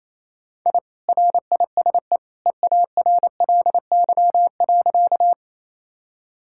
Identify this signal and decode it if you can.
Morse code, heard as a keyed tone, about 28 wpm, 700 Hz. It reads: I RISE EARLY.
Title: Sentences from Top 1000 Words - Review Part 6 - 28wpm